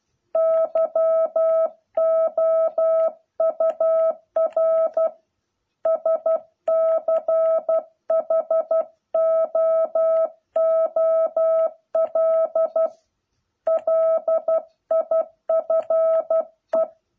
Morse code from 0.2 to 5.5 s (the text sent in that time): YOUR